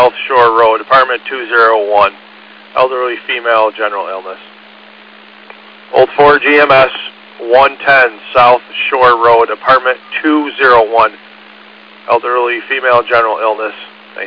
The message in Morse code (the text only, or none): none